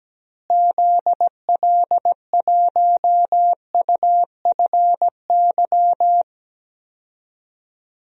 Morse code, heard as a keyed tone, about 17 wpm, 700 Hz. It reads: ZL1UFY
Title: DXCC-100 Call Signs - 6 Characters - Repeated 2x - 17wpm